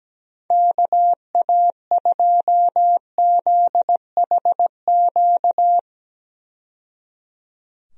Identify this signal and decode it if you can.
Morse code, heard as a keyed tone, about 17 wpm, 700 Hz. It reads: KA2ZHQ
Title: USA Call Signs - 6 Characters - 17wpm